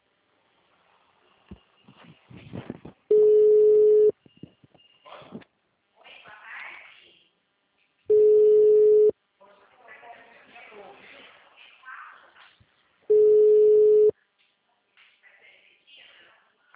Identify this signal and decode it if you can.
Ringback tone (Europe, 425 Hz), with speech.